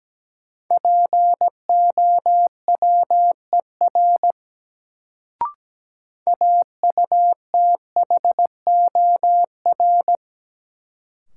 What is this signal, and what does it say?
Morse code, heard as a keyed tone, about 17 wpm, 700 Hz. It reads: POWER AUTHOR
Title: Sets of 4 Words - New Word Drill - 17wpm